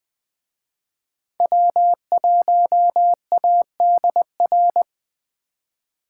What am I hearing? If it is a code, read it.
Morse code, heard as a keyed tone, about 21 wpm, 700 Hz. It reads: W1ADR
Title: Speed Racing - USA 1x3 Call Signs 17wpm